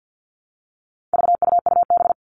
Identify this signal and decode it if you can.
Morse code, heard as a keyed tone, about 50 wpm, 700 Hz. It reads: VUUB
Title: All Cut Numbers - 4-Digit 50wpm